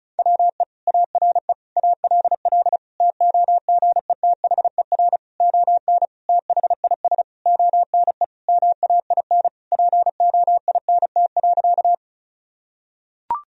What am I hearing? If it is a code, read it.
Morse code, heard as a keyed tone, about 36 wpm, 700 Hz. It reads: WE ARE ALL TOGETHER ON THIS ONE MAIN POINT.
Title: Sentences from Top 1000 Words - Review Part 2 - 35wpm